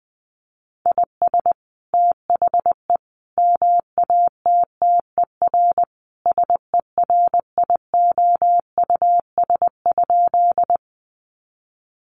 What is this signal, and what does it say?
Morse code, heard as a keyed tone, about 20 wpm, 700 Hz. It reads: IS THE MATTER SERIOUS?
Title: Sentences from Top 1100 Words 20wpm